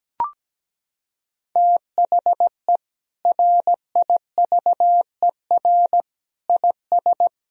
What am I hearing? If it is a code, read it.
Morse code, heard as a keyed tone, about 17 wpm, 700 Hz. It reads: THE RIVER IS
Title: Sentences from Top 1000 Words - Review Part 1 - 17wpm